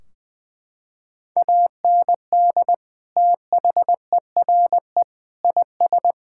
Morse code, heard as a keyed tone, about 20 wpm, 700 Hz. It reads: AND THERE IS
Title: Sentences from Top 1600 Words 20wpm